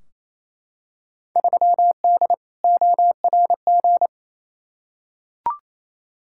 Morse code, heard as a keyed tone, about 28 wpm, 700 Hz. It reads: 3D ORG
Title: Field Day Exchanges Encore - 28wpm